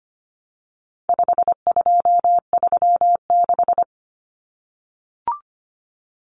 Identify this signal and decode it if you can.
Morse code, heard as a keyed tone, about 25 wpm, 700 Hz. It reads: 5236
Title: Numbers - 4-Digit 25wpm